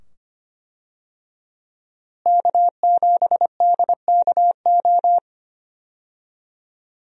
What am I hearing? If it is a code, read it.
Morse code, heard as a keyed tone, about 25 wpm, 700 Hz. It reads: K7DKO
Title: USA Call Signs - 5 Characters - Repeated 2x - 25wpm